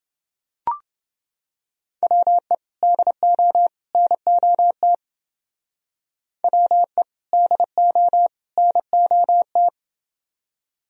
Morse code, heard as a keyed tone, about 28 wpm, 700 Hz. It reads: WE DO NOT WE DO NOT